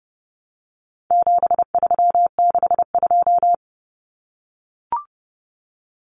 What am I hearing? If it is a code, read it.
Morse code, heard as a keyed tone, about 30 wpm, 700 Hz. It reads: 7362
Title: Numbers - 4-Digit 30wpm